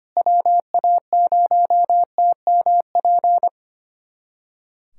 Morse code, heard as a keyed tone, about 25 wpm, 700 Hz. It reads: WA0TMP
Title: USA Callsign DE Callsign 25wpm